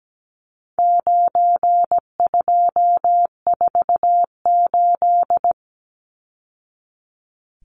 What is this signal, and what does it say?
Morse code, heard as a keyed tone, about 17 wpm, 700 Hz. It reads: 9248
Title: Numbers - 4-Digit 17wpm